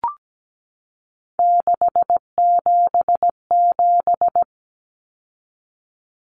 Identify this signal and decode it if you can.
Morse code, heard as a keyed tone, about 17 wpm, 700 Hz. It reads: 677
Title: Numbers - 3-Digit 17wpm